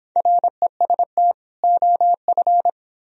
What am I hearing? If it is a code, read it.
Morse code, heard as a keyed tone, about 26 wpm, 700 Hz. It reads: REST OF